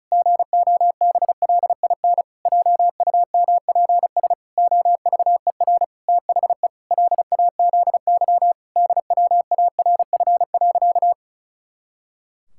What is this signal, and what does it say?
Morse code, heard as a keyed tone, about 34 wpm, 700 Hz. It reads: GOBLIN JUMPS OVER THE LAZY DWARF.
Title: Pangram Challenge v2 - 35wpm